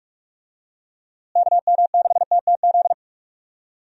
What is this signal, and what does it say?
Morse code, heard as a keyed tone, about 45 wpm, 700 Hz. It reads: KM6TTZ
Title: USA Call Signs - 6 Characters - Repeated 2x - 45wpm